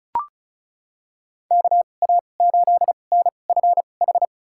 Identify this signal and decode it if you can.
Morse code, heard as a keyed tone, about 34 wpm, spaced as for 25 wpm, 700 Hz. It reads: KA8NFH
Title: USA Call Signs - 6 Characters - 2x Character Spacing - 35wpm